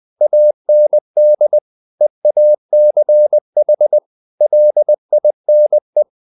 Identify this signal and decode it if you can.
Morse code, heard as a keyed tone, about 20 wpm, 600 Hz. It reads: AND EACH LINE
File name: the old man and the sea / the old man and the sea part 1 20wpm